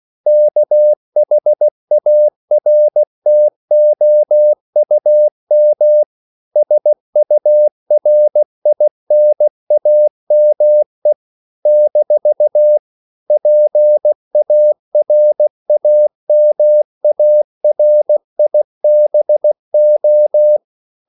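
Morse code, heard as a keyed tone, about 16 wpm, 600 Hz. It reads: KHARTOUM SURINAME - PARAMARIBO
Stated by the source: CW practice of national capitals at 16 words per minute, from ON6ZQ.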